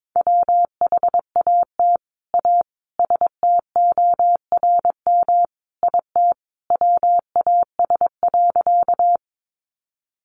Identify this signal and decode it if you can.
Morse code, heard as a keyed tone, about 22 wpm, 700 Hz. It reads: WHAT A STORM IT WAS.
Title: Sentences from Top 1500 Words 22wpm